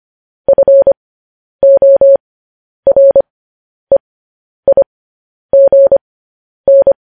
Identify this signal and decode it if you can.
Morse code, heard as a keyed tone, about 25 wpm, spaced as for 10 wpm, 550 Hz. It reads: FOREIGN
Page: news-headlines-in-morse-code-at-10-wpm